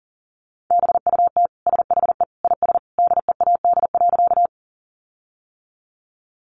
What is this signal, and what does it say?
Morse code, heard as a keyed tone, about 40 wpm, 700 Hz. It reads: BUT SHE IS DEAD.
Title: Sentences from Top 1500 Words 40wpm